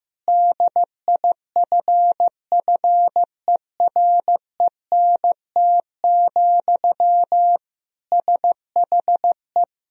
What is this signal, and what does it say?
Morse code, heard as a keyed tone, about 15 wpm, 700 Hz. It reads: DIFFERENT, SHE
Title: Sentences from Top 1000 Words - Review Part 3 - 15wpm